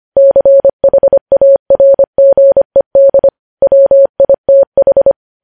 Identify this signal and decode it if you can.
Morse code, heard as a keyed tone, about 25 wpm, 550 Hz. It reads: CHARGED WITH
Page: news headlines in morse code at 25 wpm